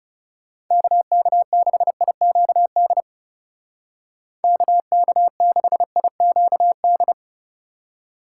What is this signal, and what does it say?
Morse code, heard as a keyed tone, about 35 wpm, 700 Hz. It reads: KK6IQD KK6IQD
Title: Speed Racing - USA 2x3 Call Signs 30wpm